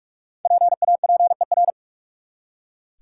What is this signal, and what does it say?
Morse code, heard as a keyed tone, about 43 wpm, 700 Hz. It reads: PAPER
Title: Five Letter Words - 45wpm